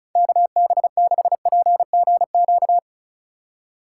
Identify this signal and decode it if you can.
Morse code, heard as a keyed tone, about 36 wpm, 700 Hz. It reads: KB6PGQ
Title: USA Call Signs - 6 Characters - 3x Character Spacing - 35wpm